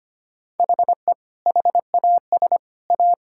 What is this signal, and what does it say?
Morse code, heard as a keyed tone, about 25 wpm, 700 Hz. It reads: HE HAS A
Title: Sentences from Top 1000 Words - Review Part 6 - 25wpm